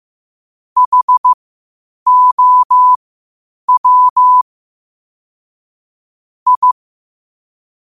Morse code, heard as a keyed tone, about 15 wpm, spaced as for 8 wpm, 1000 Hz. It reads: HOW I
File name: qotdfiles / 8wpm